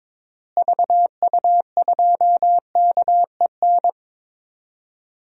Morse code, heard as a keyed tone, about 22 wpm, 700 Hz. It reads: VU2KEN
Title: DXCC-100 Call Signs - 6 Characters - Repeated 3x - 22wpm